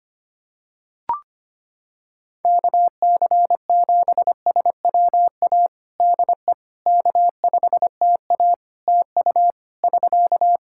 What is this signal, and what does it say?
Morse code, heard as a keyed tone, about 25 wpm, 700 Hz. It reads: KC7SWA DE K5TA TU <SK>
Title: Practice QSO - Exchange 4 - Ending - 25wpm